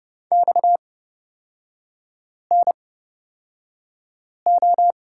Morse code, heard as a keyed tone, about 31 wpm, spaced as for 5 wpm, 700 Hz. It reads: XNO
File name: Lesson 37e - Letter x - Three Characters 30wpm